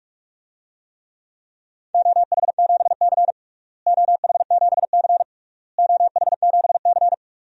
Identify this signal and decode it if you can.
Morse code, heard as a keyed tone, about 46 wpm, 700 Hz. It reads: OH7C OH7C OH7C
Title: DXCC-100 Call Signs - 4 Characters - Repeated 3x - 45wpm